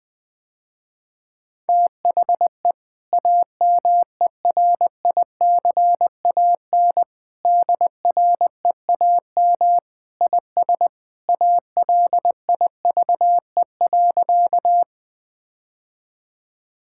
Morse code, heard as a keyed tone, about 20 wpm, 700 Hz. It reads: THE AMERICAN DREAM IS ALIVE.
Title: Sentences with 2-Word Phrases - Spoken Beginning - 20wpm